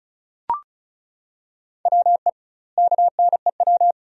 Morse code, heard as a keyed tone, about 36 wpm, 700 Hz. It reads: WE KNEW
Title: 2x Word-Spacing - Sentences from Top 400 Words - 35wpm